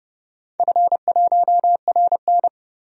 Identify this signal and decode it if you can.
Morse code, heard as a keyed tone, about 30 wpm, 700 Hz. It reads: F1RN
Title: DXCC-100 Call Signs - 4 Characters - 30wpm